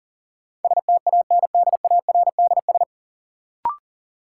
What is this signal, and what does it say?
Morse code, heard as a keyed tone, about 40 wpm, 700 Hz. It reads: STANDARDS
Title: Sentences from Top 1000 Words - Review Part 6 - New Word Drill - 40wpm